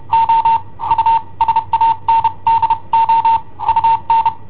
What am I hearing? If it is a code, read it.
Morse code, heard as a keyed tone, about 30 wpm, spaced as for 25 wpm, 910 Hz. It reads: O USANDO UN
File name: i7ohp / cassetta i7ohp B